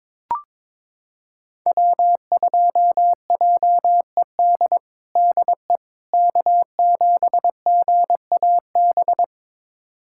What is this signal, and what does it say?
Morse code, heard as a keyed tone, about 22 wpm, 700 Hz. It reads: W2JED DE K7GAB
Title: USA Callsign DE Callsign - 5 Characters - 22wpm